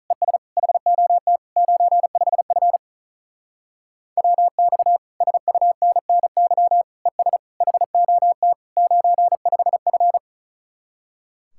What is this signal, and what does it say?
Morse code, heard as a keyed tone, about 47 wpm, 700 Hz. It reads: ES HOT 95F WX SUNNY ES HOT 95F